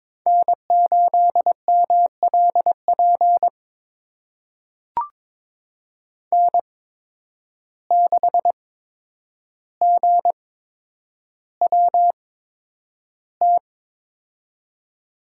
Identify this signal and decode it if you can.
Morse code, heard as a keyed tone, about 22 wpm, spaced as for 9 wpm, 700 Hz. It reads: N8MLP N6GWT
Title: USA Call Signs - 5 Characters - 8x Character Spacing - 22wpm